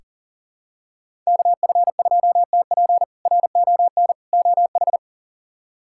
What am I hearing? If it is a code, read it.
Morse code, heard as a keyed tone, about 40 wpm, 700 Hz. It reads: KF2TP RON OH
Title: K1USN SST QSO Practice - 40wpm